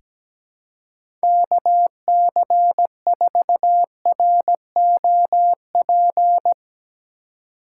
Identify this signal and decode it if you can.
Morse code, heard as a keyed tone, about 17 wpm, 700 Hz. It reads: KC4ROP
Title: USA Call Signs - 6 Characters - 3x Character Spacing - 17wpm